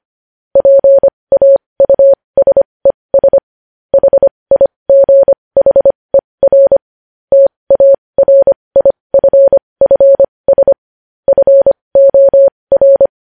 Morse code, heard as a keyed tone, about 25 wpm, spaced as for 20 wpm, 550 Hz. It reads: PAUSES HIGHER TARIFFS FOR
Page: news headlines in morse code at 20 wpm